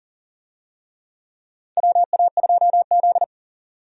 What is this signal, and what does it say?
Morse code, heard as a keyed tone, about 40 wpm, 700 Hz. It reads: WA2Z
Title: USA Call Signs - 4 Characters - 40wpm